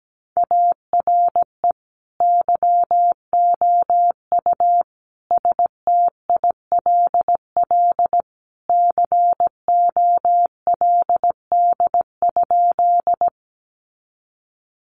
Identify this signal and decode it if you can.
Morse code, heard as a keyed tone, about 17 wpm, 700 Hz. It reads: ARE YOU STILL COLD?
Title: Sentences from Top 400 Words 17wpm